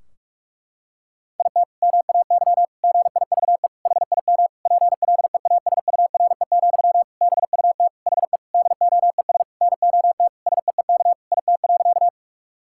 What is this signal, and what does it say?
Morse code, heard as a keyed tone, about 42 wpm, 700 Hz. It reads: IT MAY GIVE HIM PLEASURE, BUT HE DOES NOT SEEK IT.
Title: Sentences from Top 1600 Words 45wpm